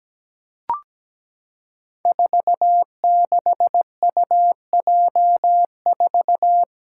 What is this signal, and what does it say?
Morse code, heard as a keyed tone, about 17 wpm, 700 Hz. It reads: 46UJ4
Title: Five Challenging Letter-Number Combinations 17wpm